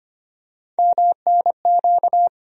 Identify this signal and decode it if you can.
Morse code, heard as a keyed tone, about 25 wpm, 700 Hz. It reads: MNQ
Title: USA Call Sign Postfixes - 3 Letters 25wpm